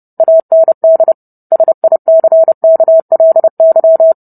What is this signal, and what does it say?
Morse code, heard as a keyed tone, about 30 wpm, 650 Hz. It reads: AND SICKLY